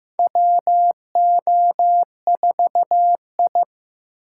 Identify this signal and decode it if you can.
Morse code, heard as a keyed tone, about 15 wpm, 700 Hz. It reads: WO4I
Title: USA Call Signs - 4 Characters - 6x Character Spacing - 15wpm